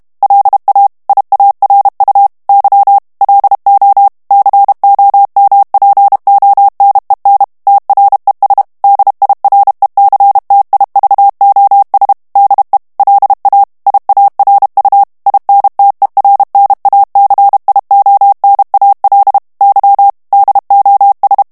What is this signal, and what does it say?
Morse code, heard as a keyed tone, about 32 wpm, 800 Hz. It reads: LA IARU Y LO COMPONEN TRES DIRECTIVOS DE LA IARU INTERNACIONAL Y DOS
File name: morse mp3 / ESPAGNOL TEXTE EN CLAIR 1 VITESSE 30 MOTS MINUTES